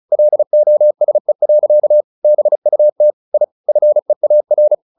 Morse code, heard as a keyed tone, about 34 wpm, 600 Hz. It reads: LOSE. BUT I FEAR
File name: the old man and the sea / the old man and the sea 1 35wpm